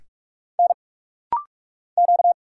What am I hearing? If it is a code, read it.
Morse code, heard as a keyed tone, about 44 wpm, 700 Hz. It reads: N Q
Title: Single Letter-Number Alt - Rapid-Fire 45wpm